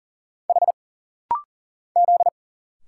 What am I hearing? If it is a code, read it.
Morse code, heard as a keyed tone, about 40 wpm, 700 Hz. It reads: H Z
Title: Single Letter-Number - Rapid-Fire 40wpm